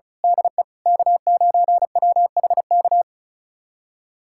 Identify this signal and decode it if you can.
Morse code, heard as a keyed tone, about 36 wpm, 700 Hz. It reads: DE K9WHK
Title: USA Callsign DE Callsign - 5 Characters - 35wpm